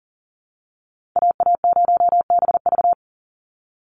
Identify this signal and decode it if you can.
Morse code, heard as a keyed tone, about 40 wpm, 700 Hz. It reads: AA0BV
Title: NA SOTA Activator Callsigns 40wpm